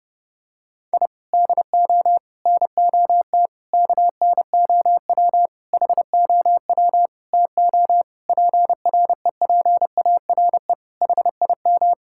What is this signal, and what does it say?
Morse code, heard as a keyed tone, about 30 wpm, 700 Hz. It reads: I DO NOT KNOW HOW TO PREPARE HIM
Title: Sentences from Top 1000 Words - Review Part 6 - 30wpm